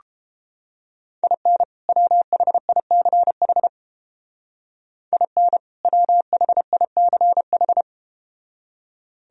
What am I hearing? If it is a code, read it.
Morse code, heard as a keyed tone, about 31 wpm, 700 Hz. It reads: IN WHICH IN WHICH